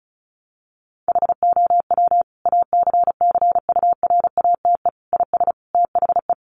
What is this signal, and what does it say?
Morse code, heard as a keyed tone, about 34 wpm, 700 Hz. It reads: HOW ACCURATE IS THE
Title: Sentences from Top 1200 Words 35wpm